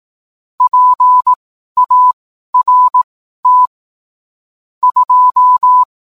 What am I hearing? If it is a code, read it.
Morse code, heard as a keyed tone, about 18 wpm, spaced as for 12 wpm, 1000 Hz. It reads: PART 2